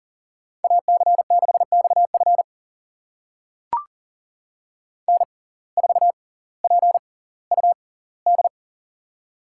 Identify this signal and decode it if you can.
Morse code, heard as a keyed tone, about 38 wpm, spaced as for 20 wpm, 700 Hz. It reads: AC6XF N4PUD
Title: USA Call Signs - 5 Characters - 6x Character Spacing - 40wpm